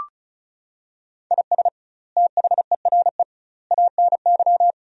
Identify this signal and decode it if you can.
Morse code, heard as a keyed tone, about 36 wpm, 700 Hz. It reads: IS THERE ANY
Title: 2x Word-Spacing - Sentences from Top 300 Words - 35wpm